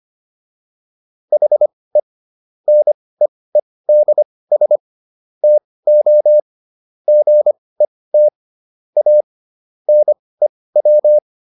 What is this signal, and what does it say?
Morse code, heard as a keyed tone, about 25 wpm, spaced as for 18 wpm, 600 Hz. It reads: HE NEEDS TO GET A NEW